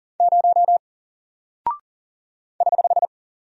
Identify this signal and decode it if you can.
Morse code, heard as a keyed tone, about 40 wpm, 700 Hz. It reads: 0 <HH>
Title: Single Character (Letter:Number:Punctuation:Prosign) Alt 40wpm